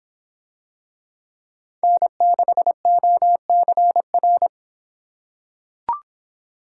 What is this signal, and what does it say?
Morse code, heard as a keyed tone, about 25 wpm, 700 Hz. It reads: N6OCR